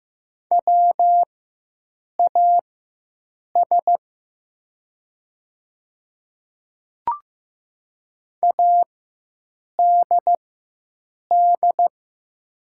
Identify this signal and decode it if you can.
Morse code, heard as a keyed tone, about 15 wpm, spaced as for 7 wpm, 700 Hz. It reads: WAS ADD
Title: 4x Character Spacing - Top 200 Words - 15wpm